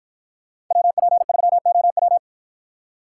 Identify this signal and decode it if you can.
Morse code, heard as a keyed tone, about 53 wpm, 700 Hz. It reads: WP3OW